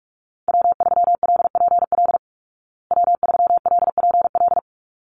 Morse code, heard as a keyed tone, about 44 wpm, 700 Hz. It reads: W3LPL W3LPL
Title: Call Signs Repeated 3x - 45wpm